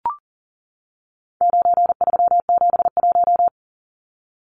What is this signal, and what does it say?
Morse code, heard as a keyed tone, about 40 wpm, 700 Hz. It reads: 9371
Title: Numbers - 4-Digit 40wpm